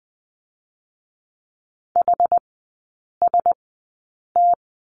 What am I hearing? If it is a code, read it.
Morse code, heard as a keyed tone, about 20 wpm, spaced as for 8 wpm, 700 Hz. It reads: HST
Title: 100 Lesson 17e - Letter u - Three Characters 20wpm